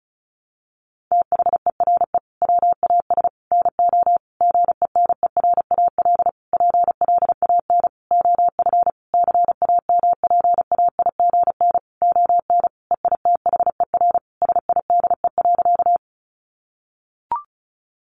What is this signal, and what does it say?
Morse code, heard as a keyed tone, about 34 wpm, 700 Hz. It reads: THERE WAS NO GENERAL PLAN OF CAMPAIGN ON EITHER SIDE.
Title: Sentences from Top 1200 Words 35wpm